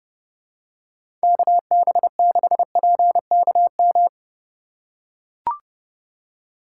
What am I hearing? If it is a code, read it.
Morse code, heard as a keyed tone, about 32 wpm, 700 Hz. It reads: KB6PKM
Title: USA Call Signs - 2x Character Spacing - 30wpm